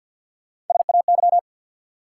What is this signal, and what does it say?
Morse code, heard as a keyed tone, about 53 wpm, 700 Hz. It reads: SAY
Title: Sets of 2 Words - New Word Drill - 50wpm